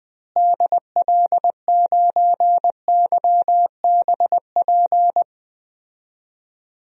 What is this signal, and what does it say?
Morse code, heard as a keyed tone, about 20 wpm, 700 Hz. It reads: DL9YBP
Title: DXCC-100 Call Signs - 6 Characters - Repeated 2x - 20wpm